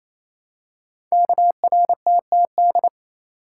Morse code, heard as a keyed tone, about 28 wpm, 700 Hz. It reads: KRTTD
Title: Code Groups - 5 Letter:Numbers - 28wpm